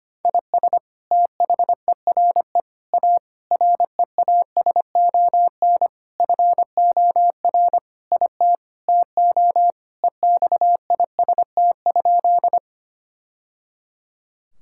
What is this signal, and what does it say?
Morse code, heard as a keyed tone, about 25 wpm, 700 Hz. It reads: IS THERE A REASON FOR IT TO EXIST?
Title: Sentences from Top 1000 Words - Review Part 5 - 25wpm